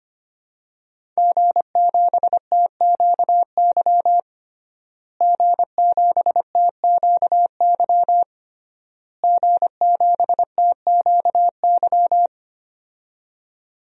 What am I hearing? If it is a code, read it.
Morse code, heard as a keyed tone, about 25 wpm, 700 Hz. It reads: G7TQY G7TQY G7TQY
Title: DXCC-100 Call Signs - 5 Characters - Repeated 3x - 25wpm